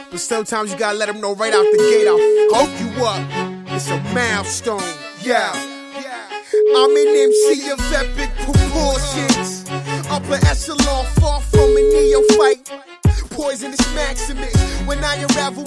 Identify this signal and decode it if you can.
Ringback tone (Europe, 425 Hz), with speech.
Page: Uvítací melodii uslyší volající, než přijmete hovor.